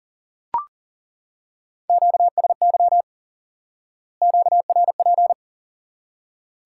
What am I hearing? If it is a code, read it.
Morse code, heard as a keyed tone, about 40 wpm, 700 Hz. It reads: QSY QRP
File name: IQE Course 3 - Lesson 89 - YAGI - Two Elements - 40wpm